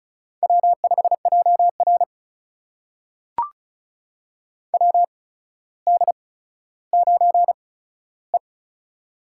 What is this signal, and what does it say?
Morse code, heard as a keyed tone, about 36 wpm, spaced as for 15 wpm, 700 Hz. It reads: W5JR WD9E
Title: USA Call Signs - 4 Characters - 8x Character Spacing - 35wpm